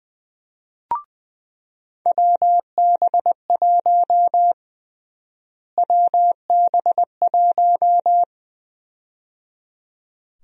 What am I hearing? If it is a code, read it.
Morse code, heard as a keyed tone, about 20 wpm, 700 Hz. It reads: WB1 WB1
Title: USA Call Sign Prefixes - Repeated 2x - 20wpm